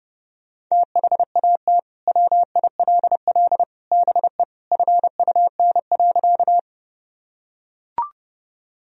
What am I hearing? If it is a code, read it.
Morse code, heard as a keyed tone, about 32 wpm, 700 Hz. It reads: THAT WILL BE FUN.
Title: Sentences from Top 1000 Words - Review Part 4 - 30wpm